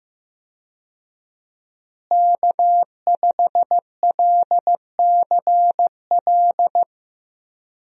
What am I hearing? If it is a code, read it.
Morse code, heard as a keyed tone, about 15 wpm, 700 Hz. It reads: K5LCL
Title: USA Call Signs - 5 Characters - Repeated 3x - 15wpm